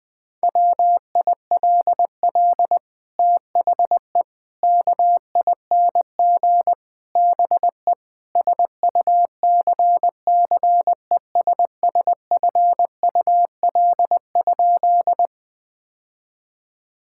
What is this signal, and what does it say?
Morse code, heard as a keyed tone, about 20 wpm, 700 Hz. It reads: WILL THE KING BE SUCCESSFUL?
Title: Sentences from Top 1000 Words - Review Part 6 - 20wpm